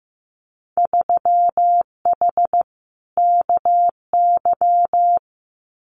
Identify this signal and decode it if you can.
Morse code, heard as a keyed tone, about 15 wpm, 700 Hz. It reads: 3H KY
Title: Winter Field Day Exchanges Encore 15wpm